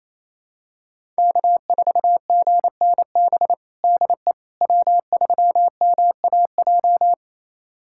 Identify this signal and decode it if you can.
Morse code, heard as a keyed tone, about 28 wpm, 700 Hz. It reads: K4GNB DE W3MAJ
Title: USA Callsign DE Callsign - 5 Characters - 28wpm